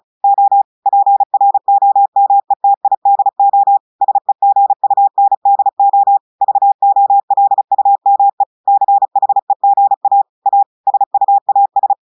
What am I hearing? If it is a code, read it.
Morse code, heard as a keyed tone, about 36 wpm, 800 Hz. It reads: O PROMETIDO SEGUNDO VOLUME CHEGA A SUAS
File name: audios / portugais / portugais texte en clair 1 vitesse 35 mots minutes